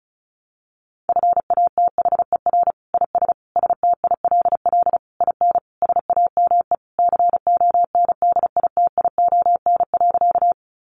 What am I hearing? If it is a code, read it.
Morse code, heard as a keyed tone, about 34 wpm, 700 Hz. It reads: FATHER IS STILL IN SAME CONDITION.
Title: Sentences from Top 1000 Words 35wpm